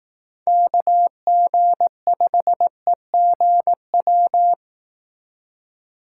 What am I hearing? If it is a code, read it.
Morse code, heard as a keyed tone, about 18 wpm, 700 Hz. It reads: KG5EGW
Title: Speed Racing - USA 2x3 Call Signs 15wpm